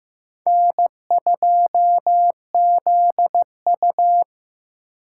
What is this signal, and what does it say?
Morse code, heard as a keyed tone, about 18 wpm, 700 Hz. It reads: N2ZU
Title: Speed Racing - USA 1x2 Call Signs 15wpm